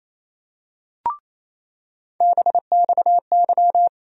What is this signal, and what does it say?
Morse code, heard as a keyed tone, about 28 wpm, 700 Hz. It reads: BXY
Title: USA Call Sign Postfixes - 3 Letters 28wpm